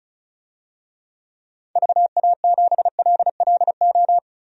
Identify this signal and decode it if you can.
Morse code, heard as a keyed tone, about 36 wpm, 700 Hz. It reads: VA7LLO
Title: DXCC-100 Call Signs - 6 Characters - Repeated 2x - 35wpm